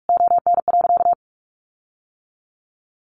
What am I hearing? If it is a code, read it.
Morse code, heard as a keyed tone, about 44 wpm, 700 Hz. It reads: ON.
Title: Sentences from Top 1300 Words 45wpm